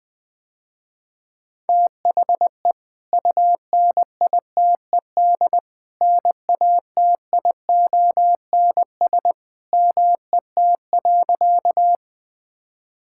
Morse code, heard as a keyed tone, about 21 wpm, 700 Hz. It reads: THE UNITED NATIONS MET.
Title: Sentences with 2-Word Phrases - Spoken Beginning - 20wpm